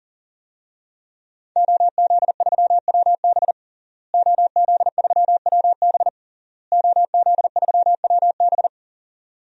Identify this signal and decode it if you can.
Morse code, heard as a keyed tone, about 40 wpm, 700 Hz. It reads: OZ3WB OZ3WB OZ3WB
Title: DXCC-100 Call Signs - 5 Characters - Repeated 3x - 40wpm